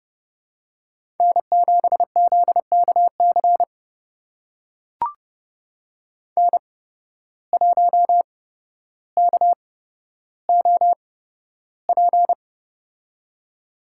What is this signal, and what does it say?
Morse code, heard as a keyed tone, about 30 wpm, spaced as for 13 wpm, 700 Hz. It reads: N7ZKC N1KOP
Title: USA Call Signs - 5 Characters - 8x Character Spacing - 30wpm